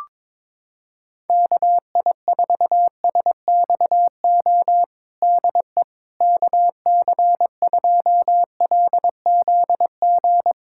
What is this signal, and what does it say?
Morse code, heard as a keyed tone, about 22 wpm, 700 Hz. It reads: KI4SXO DE KC2LZG
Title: USA Callsign DE Callsign 22wpm